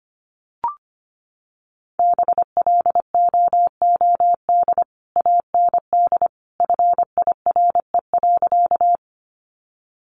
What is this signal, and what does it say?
Morse code, heard as a keyed tone, about 25 wpm, 700 Hz. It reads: BLOOD AND FIRE.
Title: Sentences from Top 1100 Words 25wpm